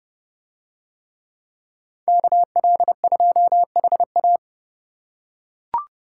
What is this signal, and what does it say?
Morse code, heard as a keyed tone, about 29 wpm, 700 Hz. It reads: KL2HA